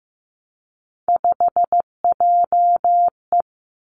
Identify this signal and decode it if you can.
Morse code, heard as a keyed tone, about 15 wpm, 700 Hz. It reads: 5JE
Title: Three Challenging Letter-Number Combinations 15wpm